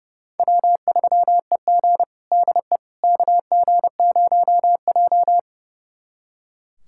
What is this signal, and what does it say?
Morse code, heard as a keyed tone, about 30 wpm, 700 Hz. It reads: W3EG DE KG0J
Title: USA Callsign DE Callsign - 4 Characters - 30wpm